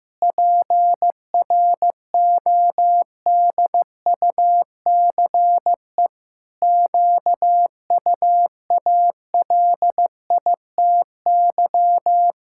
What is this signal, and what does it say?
Morse code, heard as a keyed tone, about 16 wpm, 700 Hz. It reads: PRODUCE QUALITY
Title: Sentences from Top 1000 Words - Review Part 6 - 15wpm